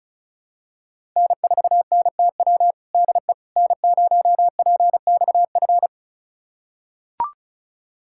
Morse code, heard as a keyed tone, about 36 wpm, 690 Hz. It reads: N4NTW DE N0PXF
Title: USA Callsign DE Callsign - 5 Characters - 35wpm